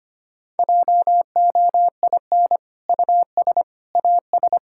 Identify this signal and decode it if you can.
Morse code, heard as a keyed tone, about 25 wpm, 700 Hz. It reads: JOIN US AS
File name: practice / 25wpm / Sentences with 2-Word Phrases - 25wpm